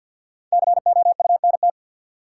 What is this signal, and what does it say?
Morse code, heard as a keyed tone, about 52 wpm, 700 Hz. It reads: COUNT
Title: Sentences from Top 1500 Words - New Word Drill Alt - 50wpm